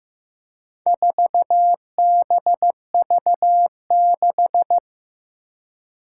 Morse code, heard as a keyed tone, about 15 wpm, 700 Hz. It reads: 4BV6
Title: Four Challenging Letter-Number Combinations 15wpm